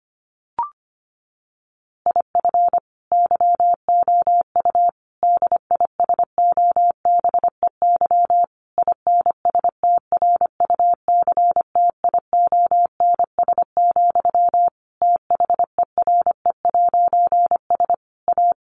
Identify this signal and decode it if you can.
Morse code, heard as a keyed tone, about 25 wpm, 700 Hz. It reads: IF YOU DISOBEY INSTRUCTIONS, THERE'S A
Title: A Sound Of Thunder 25wpm